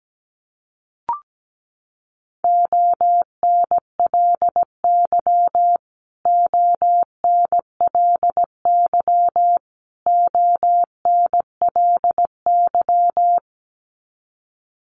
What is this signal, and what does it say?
Morse code, heard as a keyed tone, about 17 wpm, 700 Hz. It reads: ONLY ONLY ONLY
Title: Top 300 Words Repeated 17wpm